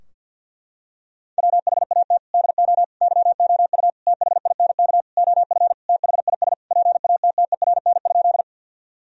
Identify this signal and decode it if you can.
Morse code, heard as a keyed tone, about 47 wpm, 700 Hz. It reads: WHAT DO YOU THINK OF THIS PATTERN?
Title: Sentences from Top 1000 Words - Review Part 2 - 50wpm